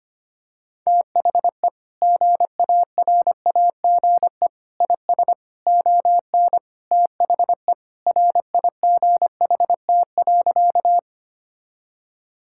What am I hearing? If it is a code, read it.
Morse code, heard as a keyed tone, about 25 wpm, 700 Hz. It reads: THE GARAGE IS ON THE RIGHT.
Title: Sentences from Top 1600 Words 25wpm